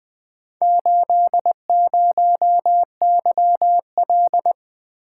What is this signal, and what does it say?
Morse code, heard as a keyed tone, about 21 wpm, 700 Hz. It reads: 80YL
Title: Four Letter-Number Combinations - 20wpm